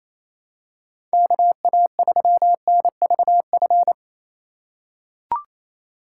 Morse code, heard as a keyed tone, about 28 wpm, 700 Hz. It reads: KA3NVF
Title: USA Call Signs - 6 Characters - 4x Character Spacing - 28wpm